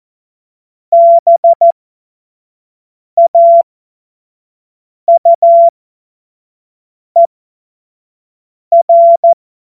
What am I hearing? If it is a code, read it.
Morse code, heard as a keyed tone, about 14 wpm, spaced as for 5 wpm, 680 Hz. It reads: BAUER
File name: Morsekurs / Witze / Witz 01. März 2023 25 F70